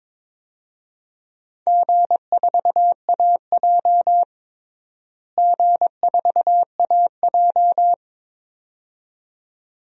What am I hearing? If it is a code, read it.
Morse code, heard as a keyed tone, about 21 wpm, 700 Hz. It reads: G4AJ G4AJ
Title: DXCC-100 Call Signs - 4 Characters - Repeated 2x - 22wpm